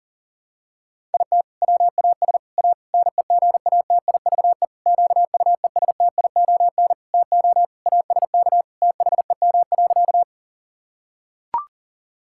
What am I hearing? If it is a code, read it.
Morse code, heard as a keyed tone, about 40 wpm, 700 Hz. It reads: IT WAS A NEGATIVE QUESTION TO ASK THEM.
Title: Sentences from Top 1000 Words - Review Part 6 - 40wpm